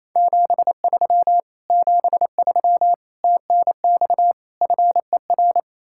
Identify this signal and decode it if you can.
Morse code, heard as a keyed tone, about 37 wpm, 700 Hz. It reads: 73 73 TNX FER